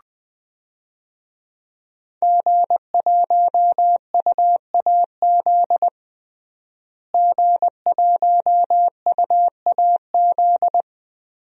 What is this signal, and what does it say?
Morse code, heard as a keyed tone, about 20 wpm, 700 Hz. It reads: G1UAZ G1UAZ
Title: DXCC-100 Call Signs - 5 Characters - Repeated 2x - 20wpm